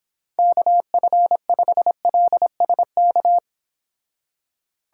Morse code, heard as a keyed tone, about 26 wpm, 700 Hz. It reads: KF5LSK